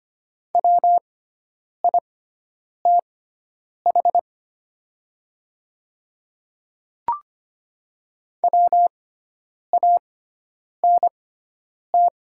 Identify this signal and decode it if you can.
Morse code, heard as a keyed tone, about 25 wpm, spaced as for 9 wpm, 700 Hz. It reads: WITH WANT
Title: 6x Character Spacing - Top 200 Words - 25wpm